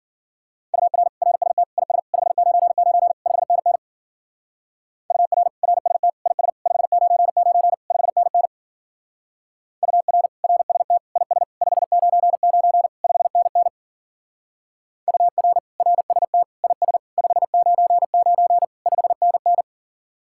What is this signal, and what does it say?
Morse code, heard as a keyed tone, about 52 wpm, 700 Hz. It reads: UR RST IS 599 5NN UR RST IS 599 5NN UR RST IS 599 5NN UR RST IS 599 5NN